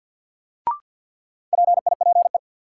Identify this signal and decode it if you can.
Morse code, heard as a keyed tone, about 49 wpm, 700 Hz. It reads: PIPE
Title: Sentences from Top 1500 Words - New Word Drill Alt - 50wpm